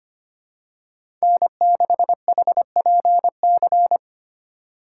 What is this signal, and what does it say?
Morse code, heard as a keyed tone, about 25 wpm, 700 Hz. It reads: N6HPC
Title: USA Call Signs - 5 Characters - 3x Character Spacing - 25wpm